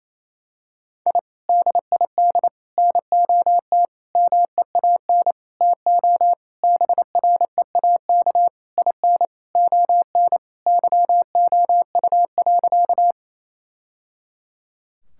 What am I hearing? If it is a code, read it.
Morse code, heard as a keyed tone, about 28 wpm, 700 Hz. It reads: I DID NOT MEAN TO BREAK IN ON YOU.
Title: Sentences from Top 1000 Words - Review Part 6 - 28wpm